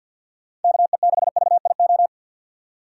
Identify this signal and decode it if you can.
Morse code, heard as a keyed tone, about 53 wpm, 700 Hz. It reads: KE6VIO
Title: USA Call Signs - 6 Characters - 6x Character Spacing - 50wpm